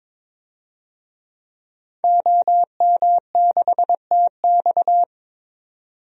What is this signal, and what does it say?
Morse code, heard as a keyed tone, about 21 wpm, 700 Hz. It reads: OM6TX
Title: DXCC-100 Call Signs - 5 Characters - 22wpm